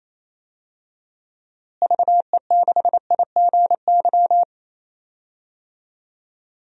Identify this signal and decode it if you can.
Morse code, heard as a keyed tone, about 28 wpm, 700 Hz. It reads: VE6IGY
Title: DXCC-100 Call Signs - 6 Characters - Repeated 2x - 28wpm